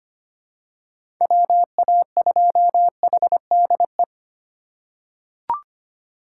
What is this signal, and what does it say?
Morse code, heard as a keyed tone, about 25 wpm, 700 Hz. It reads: WA2HDE
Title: USA Call Signs - 6 Characters - 8x Character Spacing - 25wpm